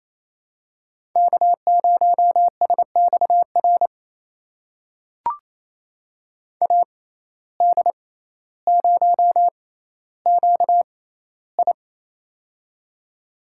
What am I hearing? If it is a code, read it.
Morse code, heard as a keyed tone, about 29 wpm, spaced as for 15 wpm, 700 Hz. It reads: K0SXR AD0QI
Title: USA Call Signs - 5 Characters - 6x Character Spacing - 28wpm